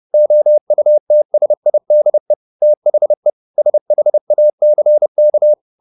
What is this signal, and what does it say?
Morse code, heard as a keyed tone, about 30 wpm, 600 Hz. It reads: OUTSIDE THE SHACK
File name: the old man and the sea / the old man and the sea 1 30wpm